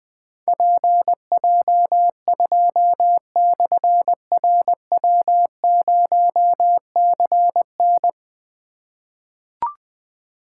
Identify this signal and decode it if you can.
Morse code, heard as a keyed tone, about 21 wpm, 700 Hz. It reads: PJ2/RW0CN
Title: Difficult Call Signs 20wpm